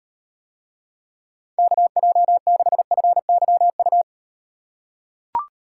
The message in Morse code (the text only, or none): KJ6FYU